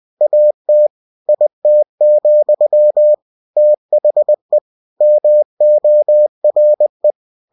Morse code, heard as a keyed tone, about 20 wpm, 600 Hz. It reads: AT IT, THE MORE
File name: cw-books / winnie-the-pooh / winnie-20wpm-page-0099